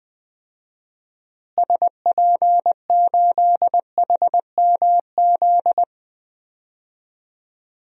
Morse code, heard as a keyed tone, about 20 wpm, 700 Hz. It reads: SP8HMZ
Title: DXCC-100 Call Signs - 6 Characters - Repeated 3x - 20wpm